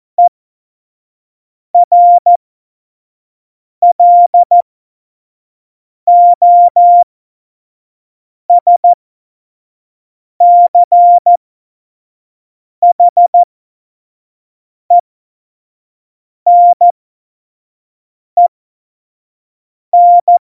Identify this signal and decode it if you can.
Morse code, heard as a keyed tone, about 14 wpm, spaced as for 5 wpm, 700 Hz. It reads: ERLOSCHENEN
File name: Morsekurs / Witze / Witz 14. August 2018, 25 F70